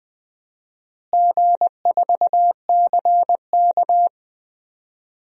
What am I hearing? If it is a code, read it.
Morse code, heard as a keyed tone, about 21 wpm, 700 Hz. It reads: G4CK
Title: DXCC-100 Call Signs - 4 Characters - Repeated 3x - 20wpm